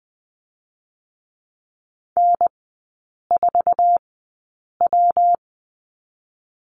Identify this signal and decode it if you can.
Morse code, heard as a keyed tone, about 20 wpm, 700 Hz. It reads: N 4 W
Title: 114 Lesson 19e - Letter w - Three Characters 20wpm